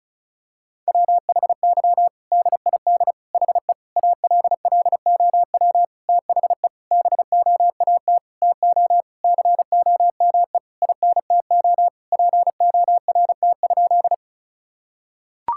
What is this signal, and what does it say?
Morse code, heard as a keyed tone, about 36 wpm, 700 Hz. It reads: WHY DID HE ALLOW THE BOAT TO COME INTO PORT?
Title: Sentences from Top 1000 Words - Review Part 5 - 35wpm